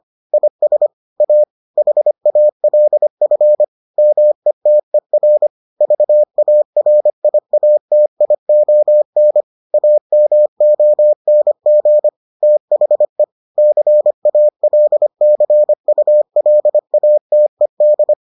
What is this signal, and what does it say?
Morse code, heard as a keyed tone, about 25 wpm, 600 Hz. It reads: IS A HALF METER VARIATION AMONG THE CALCULATED